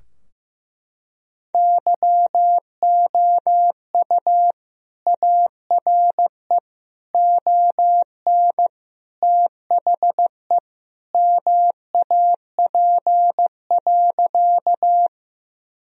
Morse code, heard as a keyed tone, about 15 wpm, 700 Hz. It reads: YOU ARE ON THE MAP.
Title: Sentences from Top 1000 Words - Review Part 2 - 15wpm